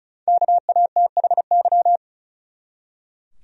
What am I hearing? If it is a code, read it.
Morse code, heard as a keyed tone, about 36 wpm, 700 Hz. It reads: KATHY
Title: USA Common Names - Max 5 Letters - 35wpm